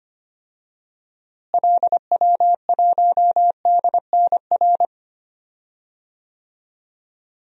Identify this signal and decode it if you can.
Morse code, heard as a keyed tone, about 25 wpm, 700 Hz. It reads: LW1DNR
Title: DXCC-100 Call Signs - 6 Characters - Repeated 2x - 25wpm